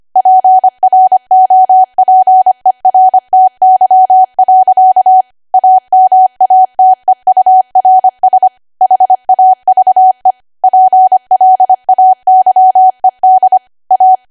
Morse code, heard as a keyed tone, about 25 wpm, 750 Hz. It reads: PROPERTY. AMATEURS HAVE PLAYED A
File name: Archive / 25 WPM / 25WPM